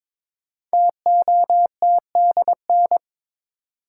Morse code, heard as a keyed tone, about 23 wpm, 700 Hz. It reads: TOTDN
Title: Code Groups - 5 Letters - 22wpm